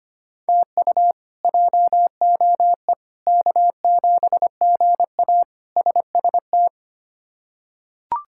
Morse code, heard as a keyed tone, about 25 wpm, 700 Hz. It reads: TU JOE K7GA SST
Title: K1USN SST QSO Practice - 25wpm